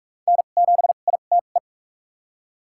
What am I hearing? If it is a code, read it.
Morse code, heard as a keyed tone, about 43 wpm, spaced as for 32 wpm, 700 Hz. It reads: N7ITE
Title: USA Call Signs - 5 Characters - 2x Character Spacing - 45wpm